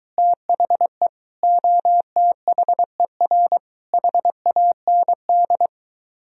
Morse code, heard as a keyed tone, about 22 wpm, 700 Hz. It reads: THE OTHER HAND